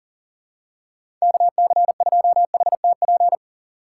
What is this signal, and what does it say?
Morse code, heard as a keyed tone, about 40 wpm, 700 Hz. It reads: KC2HTP
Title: USA Call Signs - 6 Characters - 6x Character Spacing - 40wpm